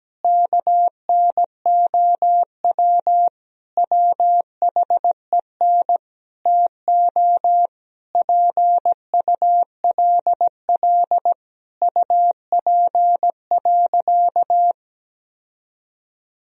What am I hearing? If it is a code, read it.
Morse code, heard as a keyed tone, about 17 wpm, 700 Hz. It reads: KNOW WHEN TO PULL UP.
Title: Sentences from Top 1000 Words - Review Part 2 - 17wpm